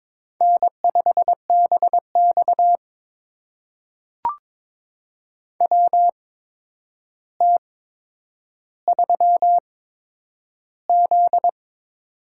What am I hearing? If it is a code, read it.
Morse code, heard as a keyed tone, about 22 wpm, spaced as for 9 wpm, 700 Hz. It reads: N5BX WT3Z
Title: USA Call Signs - 4 Characters - 8x Character Spacing - 22wpm